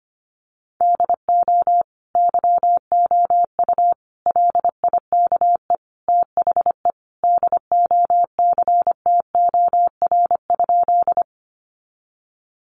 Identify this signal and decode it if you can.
Morse code, heard as a keyed tone, about 25 wpm, 700 Hz. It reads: DO YOU LIKE THE DOCTOR?
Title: Sentences from Top 1500 Words 25wpm